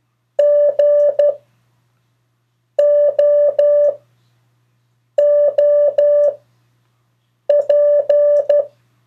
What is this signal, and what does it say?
Morse code, heard as a keyed tone, about 12 wpm, 580 Hz. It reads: G O O P
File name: 3of4-words-2